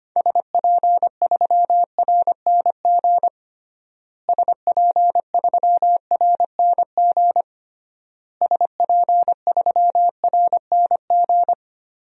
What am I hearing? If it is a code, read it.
Morse code, heard as a keyed tone, about 25 wpm, 700 Hz. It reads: SP3RNG SP3RNG SP3RNG
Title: DXCC-100 Call Signs - 6 Characters - Repeated 3x - 25wpm